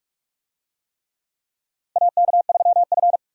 Morse code, heard as a keyed tone, about 46 wpm, 700 Hz. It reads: AK3F
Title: USA Call Signs - 4 Characters - 45wpm